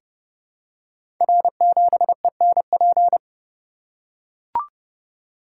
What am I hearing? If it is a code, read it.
Morse code, heard as a keyed tone, about 30 wpm, 700 Hz. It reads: R7ENP
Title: Code Groups - 5 Letter:Numbers - 30wpm